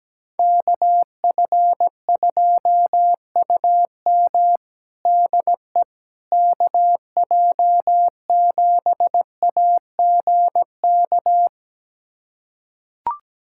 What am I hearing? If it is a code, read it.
Morse code, heard as a keyed tone, about 17 wpm, 700 Hz. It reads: KF2UM DE KJ7AGK
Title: USA Callsign DE Callsign 17wpm